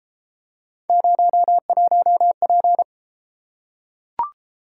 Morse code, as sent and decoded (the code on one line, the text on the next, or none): ----- .---- .--.
01P